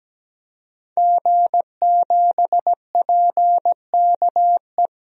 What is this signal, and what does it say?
Morse code, heard as a keyed tone, about 17 wpm, 700 Hz. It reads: G7PKE
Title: DXCC-100 Call Signs - 5 Characters - Repeated 3x - 17wpm